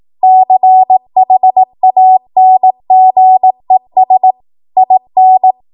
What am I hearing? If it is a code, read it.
Morse code, heard as a keyed tone, about 18 wpm, 750 Hz. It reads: CHANGES IN
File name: Morse / 18WPM